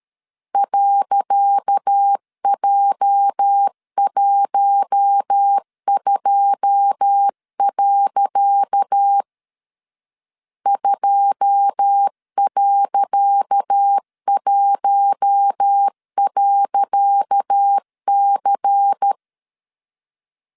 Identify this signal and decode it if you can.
Morse code, heard as a keyed tone, about 13 wpm, 800 Hz. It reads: .J12. 2.1.C
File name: LEZIONI / 10 SIGNS / 10C DOT 12wpm 2451